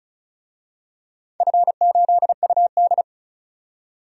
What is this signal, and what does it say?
Morse code, heard as a keyed tone, about 34 wpm, 700 Hz. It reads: F8UD
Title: DXCC-100 Call Signs - 4 Characters - 35wpm